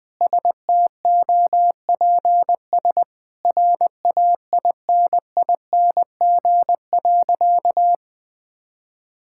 Morse code, heard as a keyed tone, about 20 wpm, 700 Hz. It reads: STOPS RAINING.
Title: Sentences with 2-Word Phrases - Spoken Beginning - 20wpm